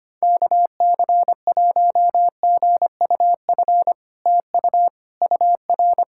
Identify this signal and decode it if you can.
Morse code, heard as a keyed tone, about 25 wpm, 700 Hz. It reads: KC1GUF TU UR
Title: SOTA QSO Practice 25wpm